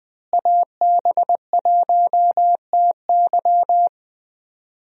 Morse code, heard as a keyed tone, about 20 wpm, 700 Hz. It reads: AB1TY
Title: USA Call Signs - 5 Characters - Repeated 2x - 20wpm